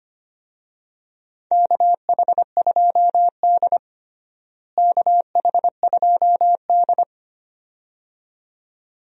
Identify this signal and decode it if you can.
Morse code, heard as a keyed tone, about 25 wpm, 700 Hz. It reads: KH2D KH2D
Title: USA Call Signs - 4 Characters - Repeated 2x - 25wpm